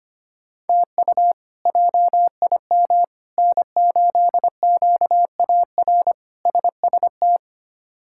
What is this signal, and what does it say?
Morse code, heard as a keyed tone, about 25 wpm, 700 Hz. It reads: TU JIM N8QAR SST
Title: K1USN SST QSO Practice - 25wpm